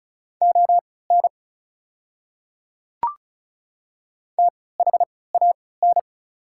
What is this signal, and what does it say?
Morse code, heard as a keyed tone, about 36 wpm, spaced as for 20 wpm, 690 Hz. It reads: ON THAN
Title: 3x Character Spacing - Top 100 Words - 35wpm